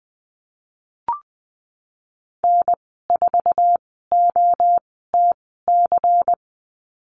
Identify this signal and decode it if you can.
Morse code, heard as a keyed tone, about 20 wpm, spaced as for 15 wpm, 700 Hz. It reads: N4OTC
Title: 154 Lesson 25g - Letter p - Call Signs 20wpm